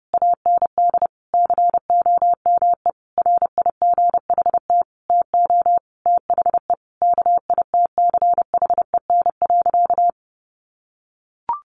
Morse code, heard as a keyed tone, about 28 wpm, 700 Hz. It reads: AND COME RIGHT TO THE KITCHEN.
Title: Sentences from Top 1400 Words 30wpm